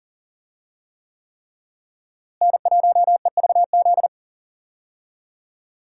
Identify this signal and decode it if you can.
Morse code, heard as a keyed tone, about 40 wpm, 700 Hz. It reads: N1EVZ
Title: USA Call Signs - 5 Characters - Repeated 3x - 40wpm